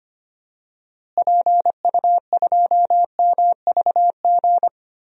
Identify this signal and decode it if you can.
Morse code, heard as a keyed tone, about 25 wpm, 700 Hz. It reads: PU2MVG
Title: DXCC-100 Call Signs - 6 Characters - Repeated 2x - 25wpm